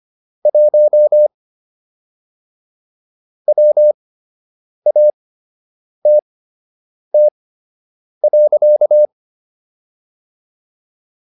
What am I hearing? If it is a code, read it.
Morse code, heard as a keyed tone, about 25 wpm, spaced as for 8 wpm, 600 Hz. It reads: 1 WATT.